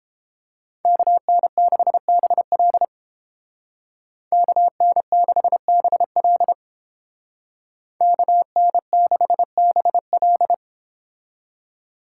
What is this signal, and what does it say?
Morse code, heard as a keyed tone, about 29 wpm, 700 Hz. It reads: KN6BL KN6BL KN6BL